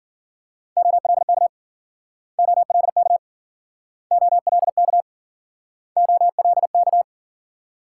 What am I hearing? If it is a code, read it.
Morse code, heard as a keyed tone, about 56 wpm, 700 Hz. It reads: OLK OLK OLK OLK